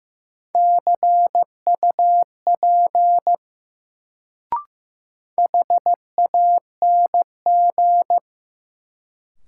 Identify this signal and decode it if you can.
Morse code, heard as a keyed tone, about 15 wpm, 700 Hz. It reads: CUP HANG
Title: Sentences from Top 1600 Words - New Word Drill - 15wpm